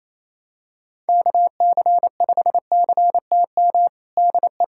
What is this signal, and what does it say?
Morse code, heard as a keyed tone, about 28 wpm, 700 Hz. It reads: KC5CTM DE
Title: USA Callsign DE Callsign - 6 Characters - 28wpm